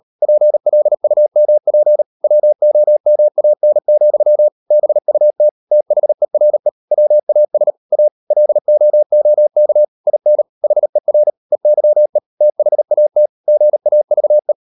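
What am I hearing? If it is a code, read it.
Morse code, heard as a keyed tone, about 38 wpm, 600 Hz. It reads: PLUMP WOMAN, BUT THERE WAS A LOOK IN HER EYE THAT GAVE
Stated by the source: Morse code practice at 38 WPM, sending Ken Follett's novel Never in English, from CW med Gnister.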